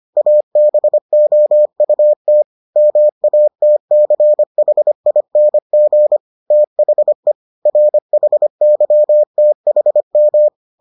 Morse code, heard as a keyed tone, about 25 wpm, 600 Hz. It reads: ABOUT MATCHING THE RHYTHM